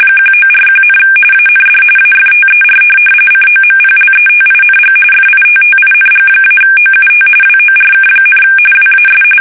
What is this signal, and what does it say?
RTTY, about 45.45 baud. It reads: 1941, AND AFTER COMPLETING FITTING-OUT WORK, WAS
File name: fsk850 / USS-Massachusetts